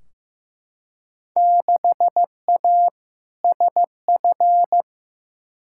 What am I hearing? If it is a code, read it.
Morse code, heard as a keyed tone, about 15 wpm, 700 Hz. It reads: 6A SF
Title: Field Day Exchanges - 15wpm